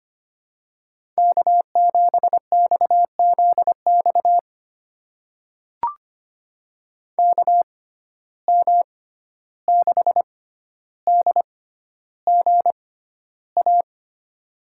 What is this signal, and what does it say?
Morse code, heard as a keyed tone, about 25 wpm, spaced as for 12 wpm, 700 Hz. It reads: K7XZX KM6DGA
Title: USA Call Signs - 6x Character Spacing - 25wpm